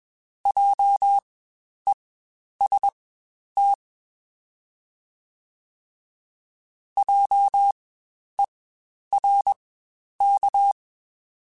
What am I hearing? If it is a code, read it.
Morse code, heard as a keyed tone, about 21 wpm, spaced as for 10 wpm, 800 Hz. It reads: JEST JERK
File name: Spaced Alphas / Spaced Alphas mp3s / Js sp @20